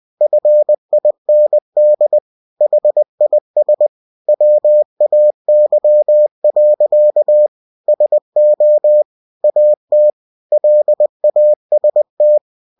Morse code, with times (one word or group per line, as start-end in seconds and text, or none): 0.2-2.2 FIND
2.6-3.9 HIS
4.3-7.5 WAY.
7.9-9.0 SO
9.4-10.1 AT
10.5-12.4 LAST